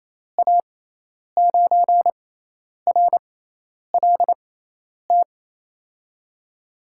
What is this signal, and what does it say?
Morse code, heard as a keyed tone, about 27 wpm, spaced as for 10 wpm, 700 Hz. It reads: A9RLT
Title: USA Call Signs - 6x Character Spacing - 28wpm